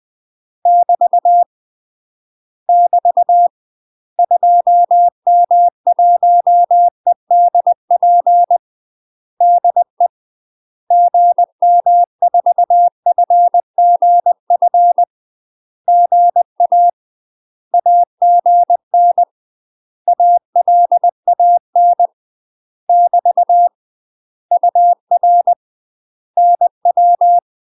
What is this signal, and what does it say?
Morse code, heard as a keyed tone, about 20 wpm, 700 Hz. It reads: <BT> <BT> 2M1EDP DE GM4FGF GA AGN ALAN <BT> UR NW
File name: UK MASTER QSOs TRAINING / 20WPM / UK MASTER QSOs 0070